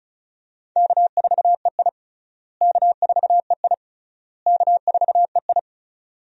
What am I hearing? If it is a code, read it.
Morse code, heard as a keyed tone, about 36 wpm, 700 Hz. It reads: K4EI K4EI K4EI
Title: USA Call Signs - 4 Characters - Repeated 3x - 35wpm